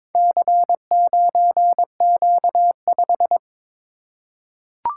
Morse code, heard as a keyed tone, about 22 wpm, 700 Hz. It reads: C9Q5
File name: practice / 22wpm / Four Letter-Number Combinations - 22wpm